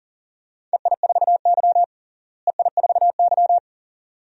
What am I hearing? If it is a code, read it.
Morse code, heard as a keyed tone, about 37 wpm, 700 Hz. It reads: EI4Y EI4Y
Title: DXCC-100 Call Signs - 4 Characters - Repeated 2x - 40wpm